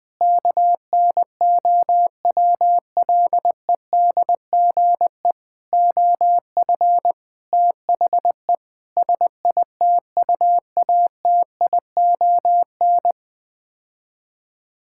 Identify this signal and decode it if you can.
Morse code, heard as a keyed tone, about 20 wpm, 700 Hz. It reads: KNOWLEDGE OF THE SITUATION
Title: Sets of 9 Words - 20wpm